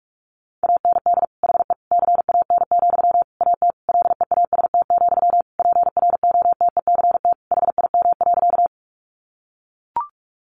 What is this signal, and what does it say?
Morse code, heard as a keyed tone, about 44 wpm, 700 Hz. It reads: AND HE CAN, AT LEAST, PROTECT HIM.
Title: Sentences from Top 900 Words 45wpm